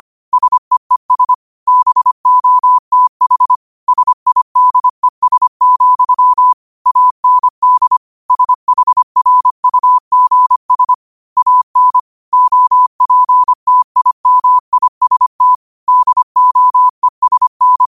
Morse code, heard as a keyed tone, about 25 wpm, 1000 Hz. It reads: SEES BOTH SIDES, AND SHRUGS AN OPTIMIST DOESN
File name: qotdfiles / 25wpm